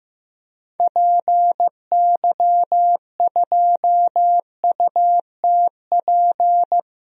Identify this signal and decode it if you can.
Morse code, heard as a keyed tone, about 15 wpm, 700 Hz. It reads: PY2UTP
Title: DXCC-100 Call Signs - 6 Characters - 15wpm